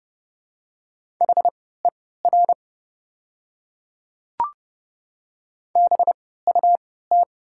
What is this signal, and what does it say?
Morse code, heard as a keyed tone, about 30 wpm, spaced as for 17 wpm, 700 Hz. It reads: HER BUT
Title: 3x Character Spacing - Top 100 Words - 30wpm